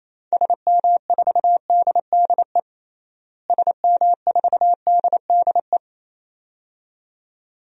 Morse code, heard as a keyed tone, about 28 wpm, 700 Hz. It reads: SM4DDE SM4DDE
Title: DXCC-100 Call Signs - 6 Characters - Repeated 2x - 28wpm